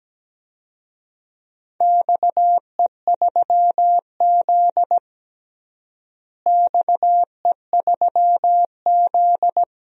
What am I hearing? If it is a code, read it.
Morse code, heard as a keyed tone, about 17 wpm, 700 Hz. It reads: XE3Z XE3Z
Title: DXCC-100 Call Signs - 4 Characters - Repeated 2x - 17wpm